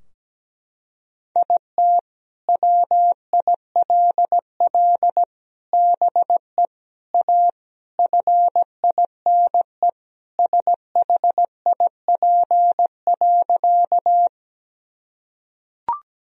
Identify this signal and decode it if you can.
Morse code, heard as a keyed tone, about 17 wpm, 700 Hz. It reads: IT WILL BE A FINE SHIP.
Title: Sentences from Top 1000 Words - Review Part 2 - 17wpm